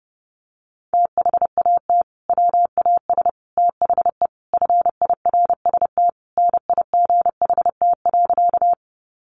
Morse code, heard as a keyed tone, about 30 wpm, 700 Hz. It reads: THAT WAS THE FIRST NIGHT.
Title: Sentences from Top 300 Words 30wpm